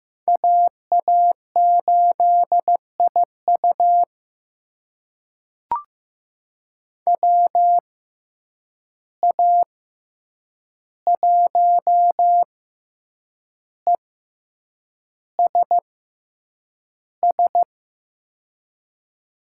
Morse code, heard as a keyed tone, about 15 wpm, spaced as for 7 wpm, 700 Hz. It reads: AA8IU WA1ESS
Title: USA Call Signs - 6x Character Spacing - 15wpm